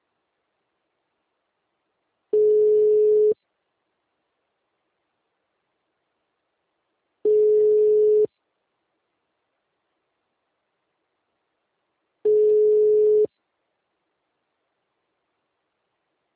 Ringback tone (Europe, 425 Hz).